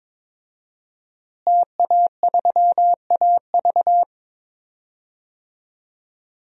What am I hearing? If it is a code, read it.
Morse code, heard as a keyed tone, about 22 wpm, 700 Hz. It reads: TA3AV
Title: DXCC-100 Call Signs - 5 Characters - Repeated 2x - 22wpm